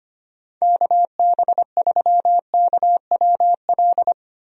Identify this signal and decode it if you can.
Morse code, heard as a keyed tone, about 25 wpm, 700 Hz. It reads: KB3KWL
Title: USA Call Signs - 6 Characters - 25wpm